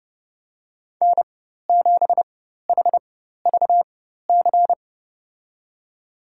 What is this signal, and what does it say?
Morse code, heard as a keyed tone, about 32 wpm, spaced as for 14 wpm, 700 Hz. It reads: N7HVC
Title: USA Call Signs - 4x Character Spacing - 30wpm